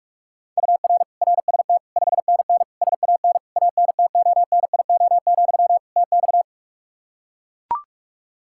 Morse code, heard as a keyed tone, about 42 wpm, 700 Hz. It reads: UR RST 5NN SAN ANTONIO, TX
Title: Practice QSO - Exchange 1 - Beginning - 45wpm